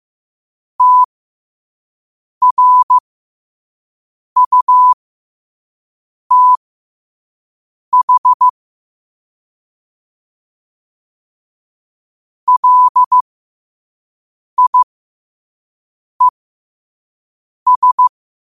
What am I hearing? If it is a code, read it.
Morse code, heard as a keyed tone, about 15 wpm, spaced as for 5 wpm, 1000 Hz. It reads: TRUTH LIES
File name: qotdfiles / 5wpm